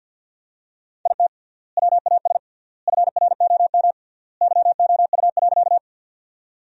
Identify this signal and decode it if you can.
Morse code, heard as a keyed tone, about 50 wpm, 700 Hz. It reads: IT WAS FROM YOU.
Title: 3x Word-Spacing - Sentences from Top 100 Words - 50wpm